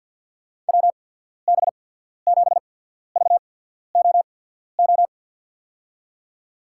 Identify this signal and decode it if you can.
Morse code, heard as a keyed tone, about 52 wpm, spaced as for 14 wpm, 700 Hz. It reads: WB7VOO
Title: USA Call Signs - 6 Characters - 8x Character Spacing - 50wpm